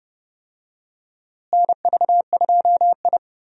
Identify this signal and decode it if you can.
Morse code, heard as a keyed tone, about 30 wpm, 700 Hz. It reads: NV2I
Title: USA Call Signs - 4 Characters - Repeated 2x - 30wpm